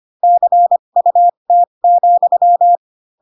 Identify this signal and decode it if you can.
Morse code, heard as a keyed tone, about 25 wpm, spaced as for 22 wpm, 700 Hz. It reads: CUT,
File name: Pinocchio / Pinocchio ch3 18 25 ik1hge